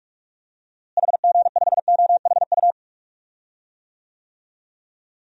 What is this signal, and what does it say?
Morse code, heard as a keyed tone, about 46 wpm, 700 Hz. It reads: HG5OHU
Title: DXCC-100 Call Signs - 6 Characters - Repeated 3x - 45wpm